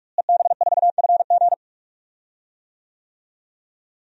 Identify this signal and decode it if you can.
Morse code, heard as a keyed tone, about 46 wpm, 700 Hz. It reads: EB4FG
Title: DXCC-100 Call Signs - 5 Characters - Repeated 3x - 45wpm